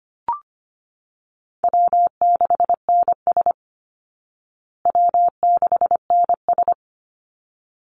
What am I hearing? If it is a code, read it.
Morse code, heard as a keyed tone, about 25 wpm, 700 Hz. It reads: W6NS W6NS
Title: Easy Call Signs Repeated 2x - 25wpm